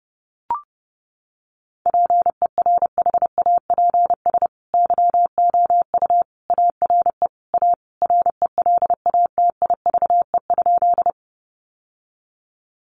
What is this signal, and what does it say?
Morse code, heard as a keyed tone, about 30 wpm, 700 Hz. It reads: PERHAPS YOU ARE A RELATIVE?
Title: Sentences from Top 1400 Words 30wpm